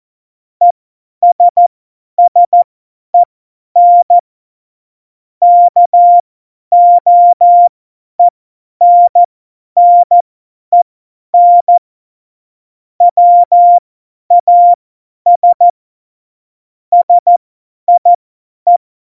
Morse code, heard as a keyed tone, about 14 wpm, spaced as for 10 wpm, 700 Hz. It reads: ESSEN KOENNEN WAS SIE
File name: Morsekurs / Witze / Witz 13 March 2018, 50 F70